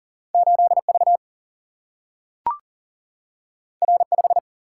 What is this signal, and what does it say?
Morse code, heard as a keyed tone, about 39 wpm, 700 Hz. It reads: 8 V R 5
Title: Single-Letter Single-Number 40wpm